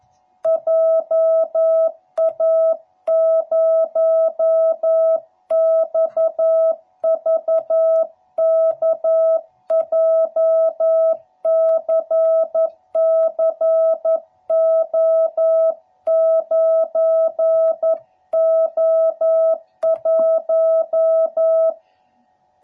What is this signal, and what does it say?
Morse code, heard as a keyed tone, about 11 wpm, 660 Hz. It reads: JA0XVK JCCO9O1
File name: jh8cbh / 2021cw / cw12b